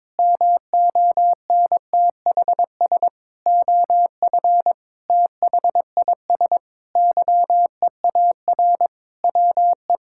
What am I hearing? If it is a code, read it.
Morse code, heard as a keyed tone, about 22 wpm, 700 Hz. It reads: MONTHS OF THIS YEAR WE
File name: practice / 22wpm / Sentences with 8-Word Phrases - 22wpm